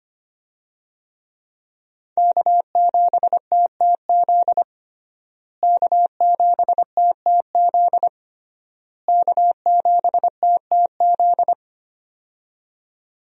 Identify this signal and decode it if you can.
Morse code, heard as a keyed tone, about 25 wpm, 700 Hz. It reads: K7TTZ K7TTZ K7TTZ
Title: USA Call Signs - 5 Characters - Repeated 3x - 25wpm